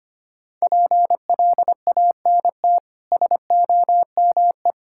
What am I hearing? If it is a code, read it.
Morse code, heard as a keyed tone, about 25 wpm, 700 Hz. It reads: PLANT SOME
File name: Easy Sentences with Hints - 25wpm